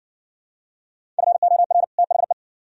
Morse code, heard as a keyed tone, about 60 wpm, 700 Hz. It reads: FOR THE